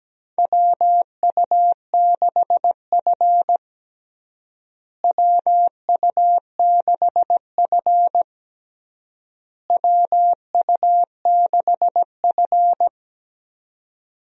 Morse code, heard as a keyed tone, about 17 wpm, 700 Hz. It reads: WU6F WU6F WU6F
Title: USA Call Signs - 4 Characters - Repeated 3x - 17wpm